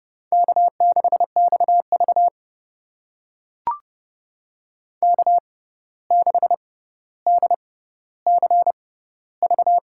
Morse code, heard as a keyed tone, about 30 wpm, spaced as for 14 wpm, 700 Hz. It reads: K6XV K6DCV
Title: USA Call Signs - 6x Character Spacing - 30wpm